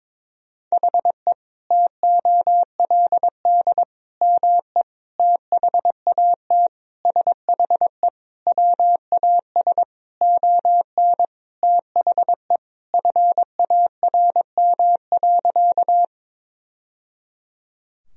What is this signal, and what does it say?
Morse code, heard as a keyed tone, about 21 wpm, 700 Hz. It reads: HE TOLD ME THAT SHE WAS ON THE FARM.
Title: Sentences from Top 1000 Words - Review Part 2 - 22wpm